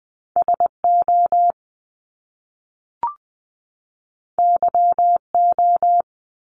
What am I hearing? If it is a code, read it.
Morse code, heard as a keyed tone, about 20 wpm, 700 Hz. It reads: SO YO
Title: Top Two Letter Combinations in Words 20wpm